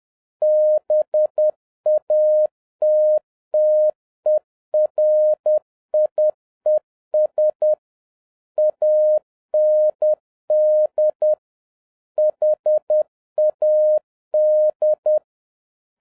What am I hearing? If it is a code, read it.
Morse code, heard as a keyed tone, about 10 wpm, 610 Hz. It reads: BATTERIES AND HAD
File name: 2011-07-mp3 / n36 THE 3.11 2 10wpm